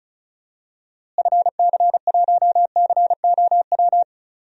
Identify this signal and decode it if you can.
Morse code, heard as a keyed tone, about 36 wpm, 700 Hz. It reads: FC1COW
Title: DXCC-100 Call Signs - 6 Characters - Repeated 3x - 35wpm